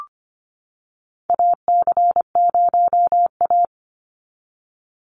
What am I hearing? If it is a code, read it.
Morse code, heard as a keyed tone, about 25 wpm, 700 Hz. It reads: AC0A
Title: NA SOTA Chaser Callsigns 25wpm